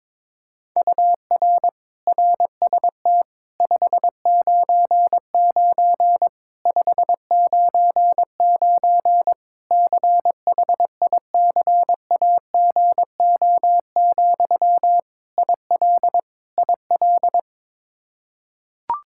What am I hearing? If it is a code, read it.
Morse code, heard as a keyed tone, about 22 wpm, 700 Hz. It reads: UR RST 599 599 CHICAGO, IL IL
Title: Practice QSO - Exchange 1 - Beginning - 22wpm